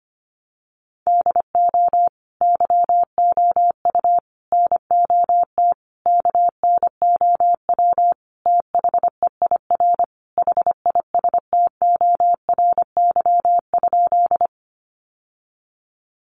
Morse code, heard as a keyed tone, about 25 wpm, 700 Hz. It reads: DO YOU NOT KNOW THEIR HISTORY?
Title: Sentences from Top 600 Words 25wpm